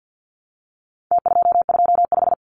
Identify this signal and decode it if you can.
Morse code, heard as a keyed tone, about 50 wpm, 700 Hz. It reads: T225
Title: Numbers - Contest Style - 4-Digit 50wpm